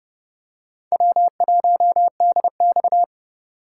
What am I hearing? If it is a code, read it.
Morse code, heard as a keyed tone, about 30 wpm, 700 Hz. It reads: W1DX
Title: K1USN SST QSO Practice - 30wpm